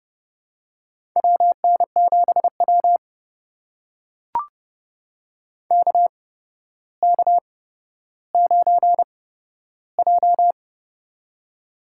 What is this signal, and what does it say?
Morse code, heard as a keyed tone, about 30 wpm, spaced as for 13 wpm, 700 Hz. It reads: WN7W KK9J
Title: USA Call Signs - 4 Characters - 8x Character Spacing - 30wpm